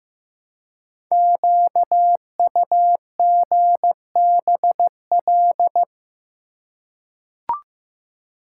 Morse code, heard as a keyed tone, about 15 wpm, 700 Hz. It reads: QUGBL
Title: Code Groups - 5 Letters - 15wpm